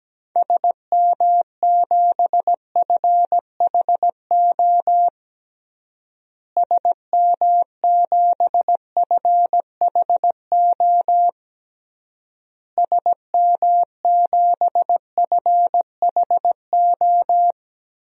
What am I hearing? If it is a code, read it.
Morse code, heard as a keyed tone, about 17 wpm, 700 Hz. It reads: SM7FHO SM7FHO SM7FHO
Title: DXCC-100 Call Signs - 6 Characters - Repeated 3x - 17wpm